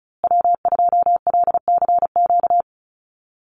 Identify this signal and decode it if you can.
Morse code, heard as a keyed tone, about 34 wpm, 700 Hz. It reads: W2LCQ
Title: Call Signs Repeated 3x - 35wpm